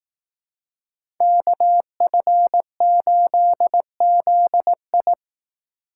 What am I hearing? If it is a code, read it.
Morse code, heard as a keyed tone, about 18 wpm, 700 Hz. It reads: KF8ZI
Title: Speed Racing - USA Call Signs 15wpm